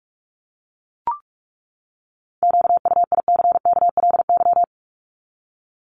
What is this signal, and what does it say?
Morse code, heard as a keyed tone, about 44 wpm, 700 Hz. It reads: QUICKLY
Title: Top 501-750 Words 45wpm